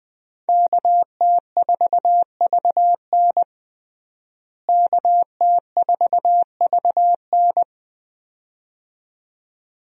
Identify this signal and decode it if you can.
Morse code, heard as a keyed tone, about 20 wpm, 700 Hz. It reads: KT4VN KT4VN
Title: USA Call Signs - 5 Characters - Repeated 2x - 20wpm